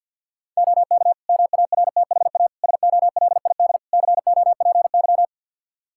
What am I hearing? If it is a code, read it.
Morse code, heard as a keyed tone, about 52 wpm, 700 Hz. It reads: OK MARTHA SOLID COPY
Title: Practice QSO - Exchange 3 - More Info - 50wpm